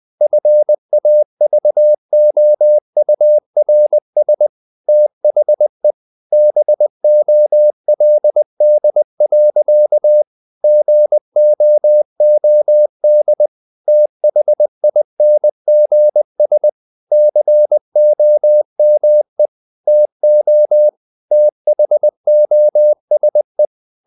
Morse code, heard as a keyed tone, about 20 wpm, 600 Hz. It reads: FAVOURS THE BOLD. GOOD THINGS COME TO THOSE
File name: CW / Practice / EnglishSayings20